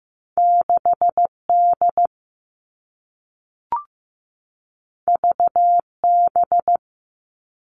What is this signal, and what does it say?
Morse code, heard as a keyed tone, about 15 wpm, 700 Hz. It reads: 6D VB
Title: All Cut Numbers - 2-Digit 15wpm